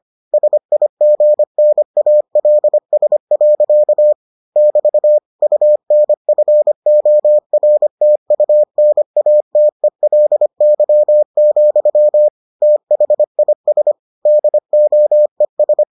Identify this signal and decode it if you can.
Morse code, heard as a keyed tone, about 25 wpm, 600 Hz. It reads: SIGNALS. <BT> UNFORTUNATELY, THIS DOES